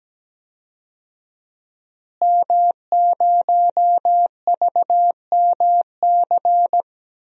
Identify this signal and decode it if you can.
Morse code, heard as a keyed tone, about 17 wpm, 700 Hz. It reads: M0VMC
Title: DXCC-100 Call Signs - 5 Characters - Repeated 2x - 17wpm